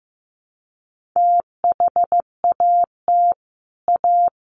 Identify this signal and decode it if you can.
Morse code, heard as a keyed tone, about 15 wpm, 700 Hz. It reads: THAT A
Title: Sets of 2 Words 15wpm